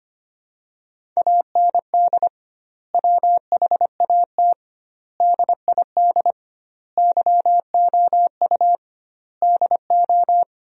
Morse code, heard as a keyed tone, about 25 wpm, 700 Hz. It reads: AND WHAT DID YOU DO
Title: 2x Word-Spacing - Sentences from Top 100 Words - 25wpm